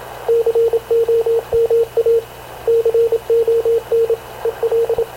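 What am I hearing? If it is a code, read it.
Morse code, heard as a keyed tone, about 26 wpm, 460 Hz, with radio noise. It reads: COMA CON EL
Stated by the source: Havana Radio / CLA sending Spanish press (PX) in Morse, recorded by Radio Officer Paul Dumesnil onboard CNAV QUEST/CZDO.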